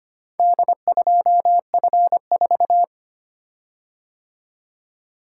Morse code, heard as a keyed tone, about 25 wpm, 700 Hz. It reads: D2F4
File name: practice / 25wpm / Four Letter-Number Combinations - 25wpm